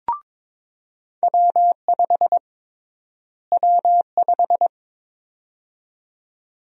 Morse code, heard as a keyed tone, about 23 wpm, 700 Hz. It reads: W5 W5
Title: USA Call Sign Prefixes - Repeated 2x - 22wpm